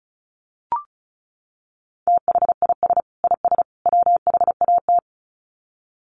Morse code, heard as a keyed tone, about 34 wpm, 700 Hz. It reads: THIS IS WHAT
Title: Sets of 3 Words 35wpm